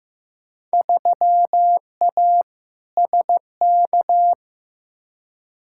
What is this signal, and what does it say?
Morse code, heard as a keyed tone, about 15 wpm, 700 Hz. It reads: 3A SK
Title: Field Day Exchanges Encore - 15wpm